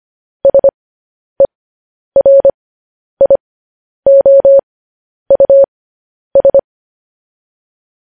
Morse code, heard as a keyed tone, about 25 wpm, spaced as for 10 wpm, 550 Hz. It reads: SERIOUS
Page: news headlines in morse code at 10 wpm